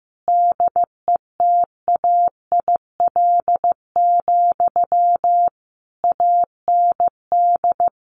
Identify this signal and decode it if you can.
Morse code, heard as a keyed tone, about 15 wpm, 700 Hz. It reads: DETAIL, AND
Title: Sentences from Top 1500 Words 15wpm